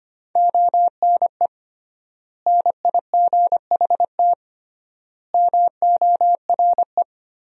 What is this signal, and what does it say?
Morse code, heard as a keyed tone, about 25 wpm, 700 Hz. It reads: ONE NIGHT MORE
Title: 3x Word-Spacing - Sentences from Top 300 Words - 25wpm